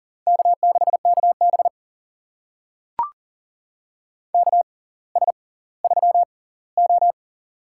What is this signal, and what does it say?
Morse code, heard as a keyed tone, about 40 wpm, spaced as for 20 wpm, 700 Hz. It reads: K6KB KS3O
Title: USA Call Signs - 4 Characters - 6x Character Spacing - 40wpm